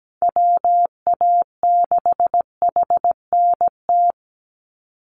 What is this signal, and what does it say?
Morse code, heard as a keyed tone, about 17 wpm, 700 Hz. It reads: WA6HNT
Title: USA Call Signs 17wpm